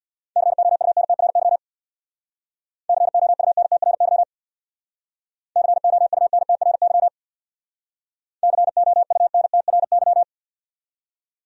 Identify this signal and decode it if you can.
Morse code, heard as a keyed tone, about 67 wpm, 700 Hz. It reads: COUNTRY COUNTRY COUNTRY COUNTRY